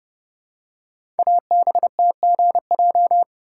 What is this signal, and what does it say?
Morse code, heard as a keyed tone, about 30 wpm, 700 Hz. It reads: ABTGJ
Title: Code Groups - 5 Letters - 30wpm